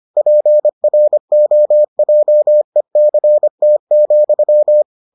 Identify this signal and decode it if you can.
Morse code, heard as a keyed tone, about 25 wpm, 600 Hz. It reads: PROJECT,